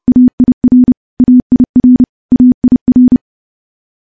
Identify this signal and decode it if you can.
Morse code, heard as a keyed tone, about 30 wpm, 250 Hz. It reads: AIR AIR AIR
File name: morse / morse-training-30WPM